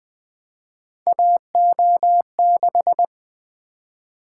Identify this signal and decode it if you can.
Morse code, heard as a keyed tone, about 20 wpm, 700 Hz. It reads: AO6
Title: Three Letter-Number Combinations - 20wpm